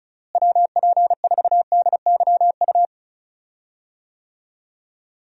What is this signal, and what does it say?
Morse code, heard as a keyed tone, about 36 wpm, 700 Hz. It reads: WP4DYU
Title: USA Call Signs - 6 Characters - Repeated 2x - 35wpm